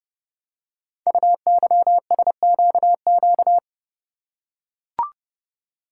Morse code, heard as a keyed tone, about 29 wpm, 690 Hz. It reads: UYSQQ
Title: Code Groups - 5 Letters - 30wpm